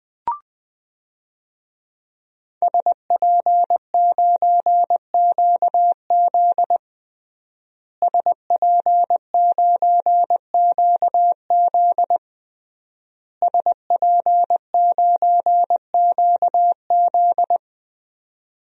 Morse code, heard as a keyed tone, about 21 wpm, 700 Hz. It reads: SP9QZ SP9QZ SP9QZ
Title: DXCC-100 Call Signs - 5 Characters - Repeated 3x - 20wpm